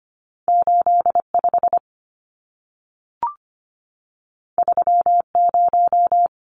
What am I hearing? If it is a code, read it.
Morse code, heard as a keyed tone, about 25 wpm, 700 Hz. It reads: 85 30
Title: Numbers - 2-Digit 25wpm